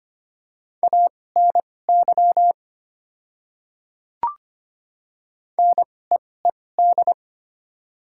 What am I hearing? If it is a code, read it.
Morse code, heard as a keyed tone, about 25 wpm, spaced as for 18 wpm, 700 Hz. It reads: ANY NEED
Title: 2x Character Spacing - Top 300 Words - 25wpm